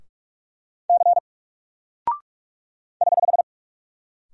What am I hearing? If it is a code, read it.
Morse code, heard as a keyed tone, about 44 wpm, 700 Hz. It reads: C <HH>
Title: Single Character (Letter:Number:Punctuation:Prosign) Alt 45wpm